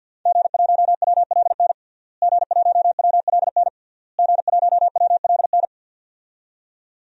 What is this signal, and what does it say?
Morse code, heard as a keyed tone, about 47 wpm, 700 Hz. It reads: G1WLN G1WLN G1WLN
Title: DXCC-100 Call Signs - 5 Characters - Repeated 3x - 50wpm